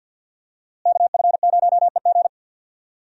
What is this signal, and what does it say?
Morse code, heard as a keyed tone, about 52 wpm, 700 Hz. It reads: KF0EG
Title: USA Call Signs - 5 Characters - Repeated 2x - 50wpm